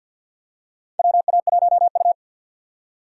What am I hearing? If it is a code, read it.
Morse code, heard as a keyed tone, about 48 wpm, 700 Hz. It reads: WA1U
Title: USA Call Signs - 4 Characters - 3x Character Spacing - 50wpm